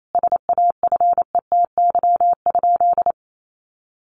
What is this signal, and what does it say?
Morse code, heard as a keyed tone, about 28 wpm, 700 Hz. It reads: SAFETY?
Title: Sentences from Top 700 Words 28wpm